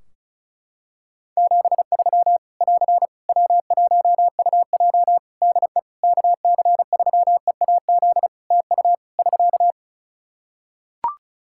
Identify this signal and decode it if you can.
Morse code, heard as a keyed tone, about 36 wpm, 700 Hz. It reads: 73 <AR> W1UJ DE KC3EAZ TU <SK>
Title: Practice QSO - Exchange 4 - Ending - 35wpm